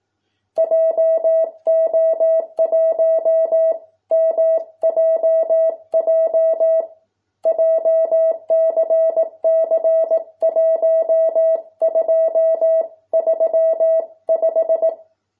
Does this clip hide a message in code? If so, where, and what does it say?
Yes: Morse code from 0.6 to 14.9 s, heard as a keyed tone, about 18 wpm, 650 Hz. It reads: JO1 MJJ JCC1235